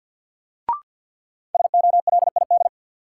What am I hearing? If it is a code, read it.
Morse code, heard as a keyed tone, about 51 wpm, 700 Hz. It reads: SOLID
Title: Sentences from Top 1500 Words - New Word Drill Alt - 50wpm